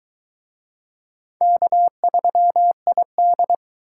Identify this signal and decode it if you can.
Morse code, heard as a keyed tone, about 22 wpm, 700 Hz. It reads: K3ID